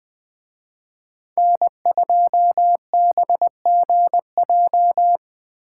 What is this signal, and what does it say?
Morse code, heard as a keyed tone, about 20 wpm, 700 Hz. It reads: N2BGJ